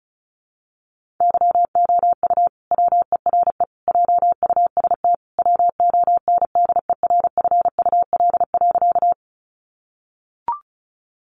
Morse code, heard as a keyed tone, about 34 wpm, 700 Hz. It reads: YOU WERE JUST WONDERFUL.
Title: Sentences from Top 1100 Words 35wpm